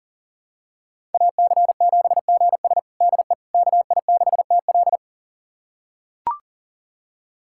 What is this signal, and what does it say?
Morse code, heard as a keyed tone, about 40 wpm, 700 Hz. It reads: AC7GS DE KI6TL
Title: USA Callsign DE Callsign - 5 Characters - 40wpm